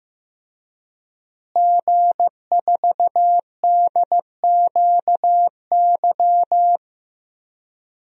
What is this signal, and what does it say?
Morse code, heard as a keyed tone, about 15 wpm, 700 Hz. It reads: G4DQY
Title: DXCC-100 Call Signs - 5 Characters - Repeated 2x - 15wpm